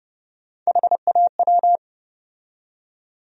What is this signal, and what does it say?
Morse code, heard as a keyed tone, about 30 wpm, 700 Hz. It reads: HAW
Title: USA Call Sign Postfixes - 3 Letters 30wpm